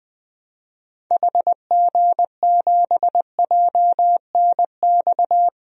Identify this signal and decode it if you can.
Morse code, heard as a keyed tone, about 21 wpm, 700 Hz. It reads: HG7JNX
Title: DXCC-100 Call Signs - 6 Characters - Repeated 3x - 20wpm